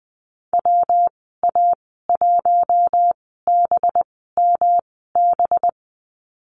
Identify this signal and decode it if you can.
Morse code, heard as a keyed tone, about 19 wpm, spaced as for 14 wpm, 700 Hz. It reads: WA1BMB
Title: 213 Lesson 34g - Number 8 - Call Signs 20wpm